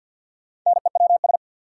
Morse code, heard as a keyed tone, about 48 wpm, 700 Hz. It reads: NEWS
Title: Sets of 5 Words - New Word Drill - 50wpm